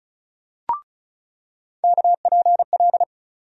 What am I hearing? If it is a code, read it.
Morse code, heard as a keyed tone, about 34 wpm, 700 Hz. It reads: KPL
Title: USA Call Sign Postfixes - 3 Letters 35wpm